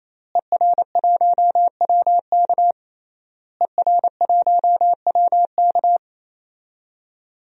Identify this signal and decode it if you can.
Morse code, heard as a keyed tone, about 28 wpm, 700 Hz. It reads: ER1WK ER1WK
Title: DXCC-100 Call Signs - 5 Characters - Repeated 3x - 28wpm